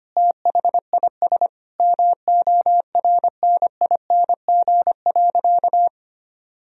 Morse code, heard as a keyed tone, about 25 wpm, 700 Hz. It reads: THIS MORNING.
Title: Sentences from Top 1000 Words - Review Part 3 - 25wpm